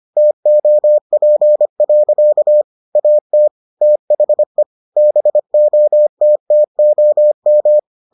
Morse code, heard as a keyed tone, about 25 wpm, 600 Hz. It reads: TOP. AT THE BOTTOM